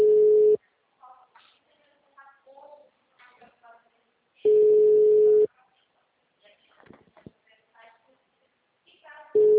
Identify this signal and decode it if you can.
Ringback tone (Europe, 425 Hz), with speech.